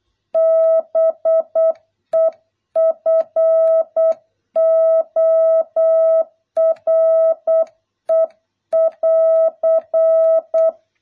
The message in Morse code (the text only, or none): BEFORE <AR>